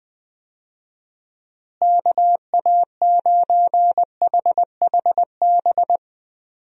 Morse code, heard as a keyed tone, about 20 wpm, 700 Hz. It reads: KA9HHB
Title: USA Call Signs - 6 Characters - Repeated 3x - 20wpm